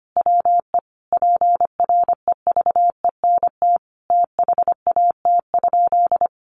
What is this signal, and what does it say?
Morse code, heard as a keyed tone, about 25 wpm, 700 Hz. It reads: WE PREVENT THAT?
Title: Sentences from Top 1100 Words 25wpm